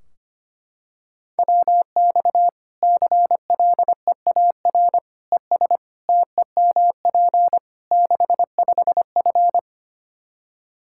Morse code, heard as a keyed tone, about 25 wpm, 700 Hz. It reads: WX CLEAR ES TEMP 65F
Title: Practice QSO - Exchange 2 - Info - 25wpm